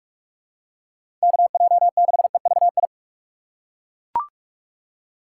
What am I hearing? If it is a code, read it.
Morse code, heard as a keyed tone, about 42 wpm, 700 Hz. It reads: KJ6EVI
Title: USA Call Signs - 3x Character Spacing - 45wpm